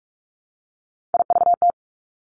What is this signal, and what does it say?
Morse code, heard as a keyed tone, about 46 wpm, 700 Hz. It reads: IVT
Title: Three Challenging Letter-Number Combinations 45wpm